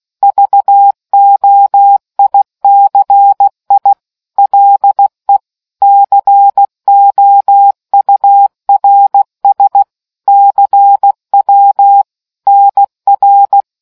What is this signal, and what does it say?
Morse code, heard as a keyed tone, about 16 wpm, 800 Hz. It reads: VOICI LE COURS CW NR